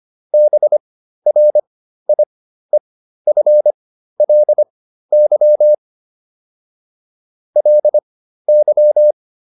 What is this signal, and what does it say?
Morse code, heard as a keyed tone, about 25 wpm, spaced as for 13 wpm, 600 Hz. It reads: BRIEFLY LY
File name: LY7 13